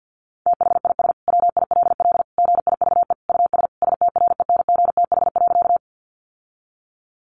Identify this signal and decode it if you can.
Morse code, heard as a keyed tone, about 50 wpm, 700 Hz. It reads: THIS WILL GIVE US STRENGTH.
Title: Sentences from Top 1300 Words 50wpm